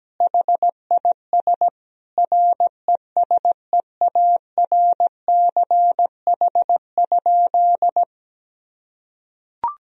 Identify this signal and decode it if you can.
Morse code, heard as a keyed tone, about 17 wpm, 700 Hz. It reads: HIS RESEARCH?
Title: Sentences from Top 1000 Words - Review Part 4 - 17wpm